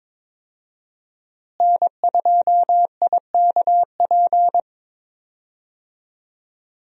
Morse code, heard as a keyed tone, about 22 wpm, 700 Hz. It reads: N2IKP
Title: USA Call Signs - 5 Characters - Repeated 3x - 22wpm